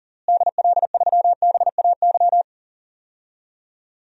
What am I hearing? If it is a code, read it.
Morse code, heard as a keyed tone, about 40 wpm, 700 Hz. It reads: DL3BAY
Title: DXCC-100 Call Signs - 6 Characters - Repeated 3x - 40wpm